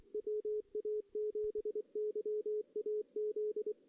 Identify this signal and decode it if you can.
Morse code, heard as a keyed tone, about 24 wpm, 440 Hz. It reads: WA7YAZ
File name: 2016 JAN CW NAQP / 2016 Jan naqp k5ot - 0-2